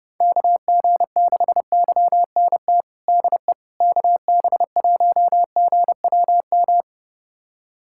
Morse code, heard as a keyed tone, about 31 wpm, 700 Hz. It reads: KG6YNT DE KB1GWM
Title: USA Callsign DE Callsign - 6 Characters - 30wpm